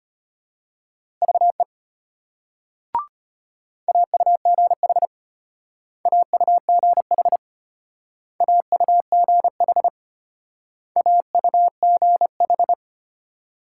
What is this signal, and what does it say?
Morse code, heard as a keyed tone, about 32 wpm, 700 Hz. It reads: VE AUGH AUGH AUGH AUGH